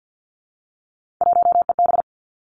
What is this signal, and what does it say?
Morse code, heard as a keyed tone, about 50 wpm, 700 Hz. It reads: 1EB
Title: Three Challenging Letter-Number Combinations 50wpm